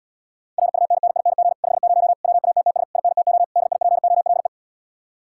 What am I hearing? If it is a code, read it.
Morse code, heard as a keyed tone, about 80 wpm, 700 Hz. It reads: RUNNING 50 WATTS INTO DIPOLE